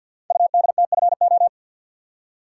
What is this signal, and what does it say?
Morse code, heard as a keyed tone, about 48 wpm, 700 Hz. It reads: UDTFO
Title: Code Groups - 5 Letters - 50wpm